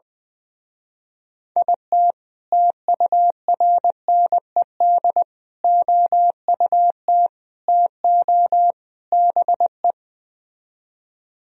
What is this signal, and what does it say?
Morse code, heard as a keyed tone, about 20 wpm, 700 Hz. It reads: IT TURNED OUT TO BE
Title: Speed-Racing - Sets of 5 Words 20wpm